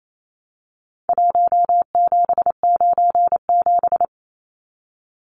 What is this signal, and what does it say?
Morse code, heard as a keyed tone, about 28 wpm, 700 Hz. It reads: 1797
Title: Numbers - Contest Style - 4-Digit 28wpm